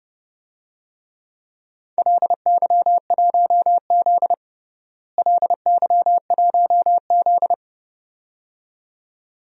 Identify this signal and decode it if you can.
Morse code, heard as a keyed tone, about 30 wpm, 700 Hz. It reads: LY1Z LY1Z
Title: DXCC-100 Call Signs - 4 Characters - Repeated 2x - 30wpm